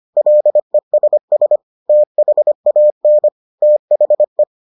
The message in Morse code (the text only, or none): LESS THAN THE